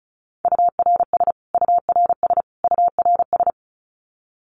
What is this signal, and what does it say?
Morse code, heard as a keyed tone, about 34 wpm, 700 Hz. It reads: URS URS URS
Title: Abbreviations Repeated 3x - 35wpm